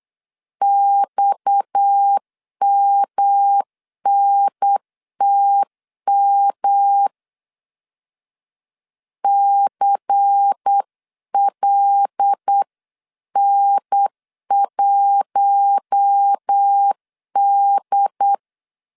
Morse code, heard as a keyed tone, about 8 wpm, 800 Hz. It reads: XMNTM CLN1D
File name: 8wpm